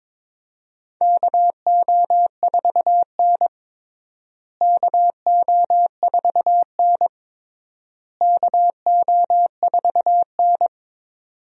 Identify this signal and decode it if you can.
Morse code, heard as a keyed tone, about 22 wpm, 700 Hz. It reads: KO4N KO4N KO4N
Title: USA Call Signs - 4 Characters - Repeated 3x - 22wpm